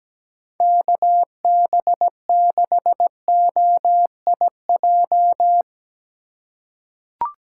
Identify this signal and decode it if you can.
Morse code, heard as a keyed tone, about 17 wpm, 700 Hz. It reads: KB6OIJ
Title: USA Call Signs - 3x Character Spacing - 17wpm